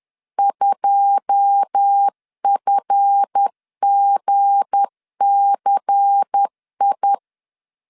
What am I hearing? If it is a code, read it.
Morse code, heard as a keyed tone, about 10 wpm, 800 Hz. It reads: 2FGCI
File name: LEZIONI / 9 2 10wpm 2736